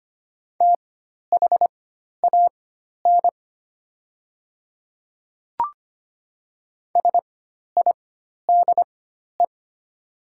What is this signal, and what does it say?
Morse code, heard as a keyed tone, about 25 wpm, spaced as for 12 wpm, 690 Hz. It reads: THAN SIDE
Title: 4x Character Spacing - Top 200 Words - 25wpm